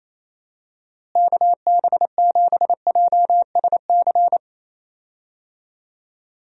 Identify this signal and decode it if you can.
Morse code, heard as a keyed tone, about 28 wpm, 700 Hz. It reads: KB7JSC
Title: USA Call Signs - 6 Characters - Repeated 2x - 28wpm